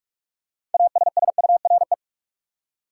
Morse code, heard as a keyed tone, about 43 wpm, 700 Hz. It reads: ASSURE
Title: Six Letter Words - 45wpm